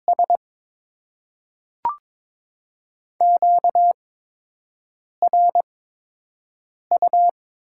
Morse code, heard as a keyed tone, about 22 wpm, spaced as for 6 wpm, 700 Hz. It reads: S QRU
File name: Top 90 QSO Elements - 8x Character Spacing - 22wpm